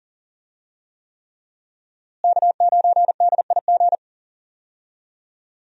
Morse code, heard as a keyed tone, about 39 wpm, 700 Hz. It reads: K9DIG
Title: USA Call Signs - 5 Characters - Repeated 3x - 40wpm